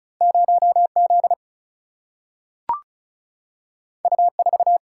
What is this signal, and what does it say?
Morse code, heard as a keyed tone, about 36 wpm, 700 Hz. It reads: 0Z U4
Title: Single-Letter Single-Number 35wpm